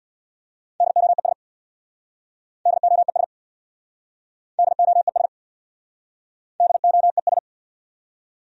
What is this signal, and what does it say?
Morse code, heard as a keyed tone, about 63 wpm, 700 Hz. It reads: DOES DOES DOES DOES